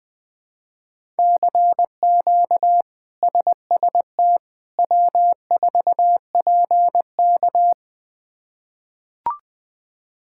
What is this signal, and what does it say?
Morse code, heard as a keyed tone, about 20 wpm, 700 Hz. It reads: CQ SST W4PK
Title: K1USN SST QSO Practice - 20wpm